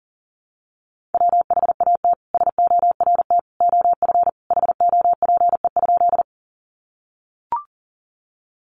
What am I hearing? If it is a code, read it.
Morse code, heard as a keyed tone, about 40 wpm, 700 Hz. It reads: WHAT SORT OF HOPE?
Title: Sentences from Top 1300 Words 40wpm